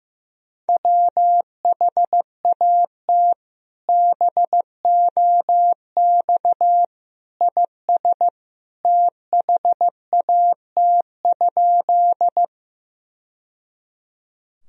Morse code, heard as a keyed tone, about 16 wpm, 700 Hz. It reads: WHAT BOX IS THAT?
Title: Sentences from Top 1000 Words - Review Part 2 - 15wpm